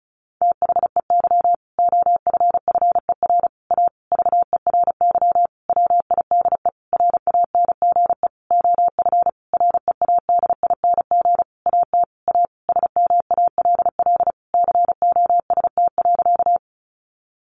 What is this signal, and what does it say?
Morse code, heard as a keyed tone, about 34 wpm, 700 Hz. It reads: THEY OFFER A VERY WIDE RANGE OF READING AT A SMALL COST.
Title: Sentences from Top 900 Words 35wpm